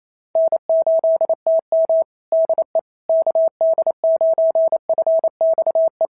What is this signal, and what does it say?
Morse code, heard as a keyed tone, about 27 wpm, 650 Hz. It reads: N8TM DE KD9FXE
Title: SOTA QSO Practice 28wpm